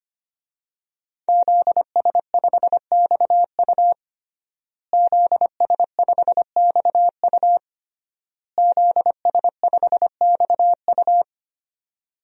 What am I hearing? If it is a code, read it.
Morse code, heard as a keyed tone, about 25 wpm, 700 Hz. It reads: ZS5XU ZS5XU ZS5XU
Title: DXCC-100 Call Signs - 5 Characters - Repeated 3x - 25wpm